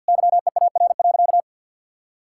Morse code, heard as a keyed tone, about 50 wpm, 700 Hz. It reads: YEAR.
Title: Sentences from Top 1600 Words 50wpm